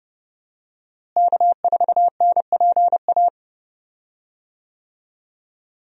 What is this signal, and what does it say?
Morse code, heard as a keyed tone, about 32 wpm, 700 Hz. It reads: K4NPA
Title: USA Call Signs - 5 Characters - Repeated 3x - 30wpm